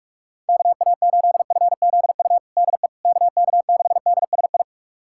Morse code, heard as a keyed tone, about 46 wpm, 700 Hz. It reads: KA8FZU DE KK6DSI
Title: USA Callsign DE Callsign - 6 Characters - 45wpm